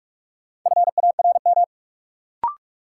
Morse code, heard as a keyed tone, about 46 wpm, 700 Hz. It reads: FARM
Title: Sentences from Top 1000 Words - Review Part 2 - New Word Drill - 45wpm